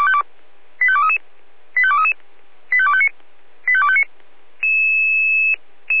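Five-tone selective call (ZVEI), heard as quick runs of five tones, tones of about 70 ms. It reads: 85233 85233 85259 85259
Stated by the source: ZVEI five-tone selective call received via SDR.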